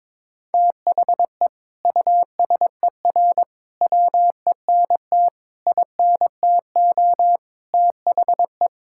Morse code, heard as a keyed tone, about 22 wpm, 700 Hz. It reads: THE USER WENT INTO THE
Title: Sentences from Top 1000 Words - Review Part 5 - 22wpm